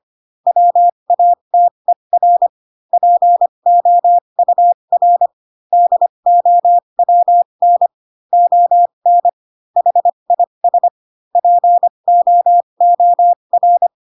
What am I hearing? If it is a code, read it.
Morse code, heard as a keyed tone, about 25 wpm, spaced as for 22 wpm, 700 Hz. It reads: WATER POUR DOWN ON HIS POOR
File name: Pinocchio / Pinocchio ch6 18 25 ik1hge